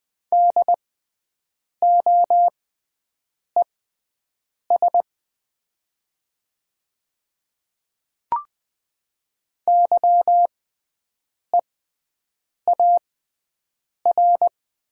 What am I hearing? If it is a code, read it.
Morse code, heard as a keyed tone, about 20 wpm, spaced as for 7 wpm, 700 Hz. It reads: DOES YEAR
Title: 6x Character Spacing - Top 300 Words - 20wpm